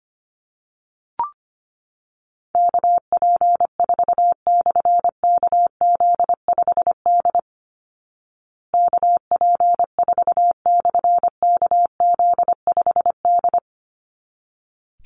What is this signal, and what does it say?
Morse code, heard as a keyed tone, about 25 wpm, 700 Hz. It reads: KP4/KZ5D KP4/KZ5D
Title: Call Signs Repeated 3x - 25wpm